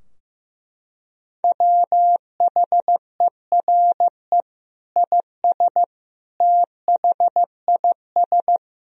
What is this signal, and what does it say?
Morse code, heard as a keyed tone, about 15 wpm, 700 Hz. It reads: WHERE IS THIS
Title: Sentences from Top 1000 Words - Review Part 3 - 15wpm